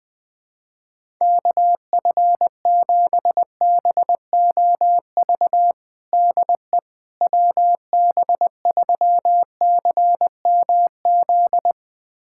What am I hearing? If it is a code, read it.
Morse code, heard as a keyed tone, about 20 wpm, 700 Hz. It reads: KF7BOV DE WB3CMZ
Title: USA Callsign DE Callsign 20wpm